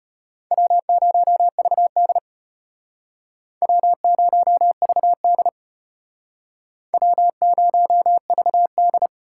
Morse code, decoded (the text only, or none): W0VD W0VD W0VD